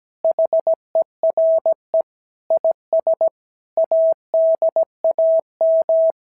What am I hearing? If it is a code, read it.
Morse code, heard as a keyed tone, about 16 wpm, 650 Hz. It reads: HERE IS ADAM
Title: SOTA QSO Practice 17wpm